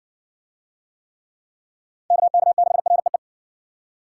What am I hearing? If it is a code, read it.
Morse code, heard as a keyed tone, about 60 wpm, 700 Hz. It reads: KK6REE